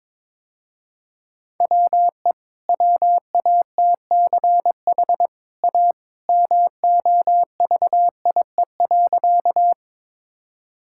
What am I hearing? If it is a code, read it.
Morse code, heard as a keyed tone, about 22 wpm, 700 Hz. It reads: WE WATCH A MOVIE.
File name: practice / 22wpm / Easy Sentences with Hints - 22wpm